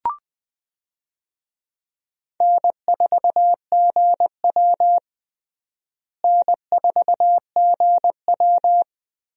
Morse code, heard as a keyed tone, about 20 wpm, 700 Hz. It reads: N4GW N4GW
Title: USA Call Signs - 4 Characters - Repeated 2x - 20wpm